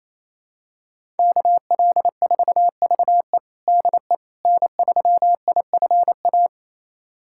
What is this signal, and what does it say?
Morse code, heard as a keyed tone, about 28 wpm, 700 Hz. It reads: KL4VE DE N3IFA
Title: USA Callsign DE Callsign - 5 Characters - 28wpm